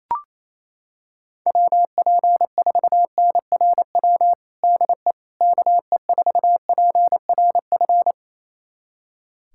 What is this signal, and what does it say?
Morse code, heard as a keyed tone, about 27 wpm, 700 Hz. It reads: WP4NRW DE KE4PRF
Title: USA Callsign DE Callsign - 6 Characters - 28wpm